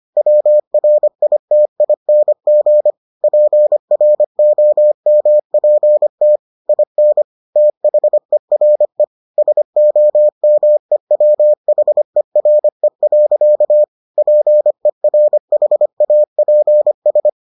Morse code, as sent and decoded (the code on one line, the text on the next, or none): .-- .-. .. - .. -. --. / .--. .-. --- -- .--. - / .. -. / - .... .-. / ... --- -- . .-- .... .-. .-.-.- / .--. .-. .... .- .--. ...
WRITING PROMPT IN THERE SOMEWHERE. PERHAPS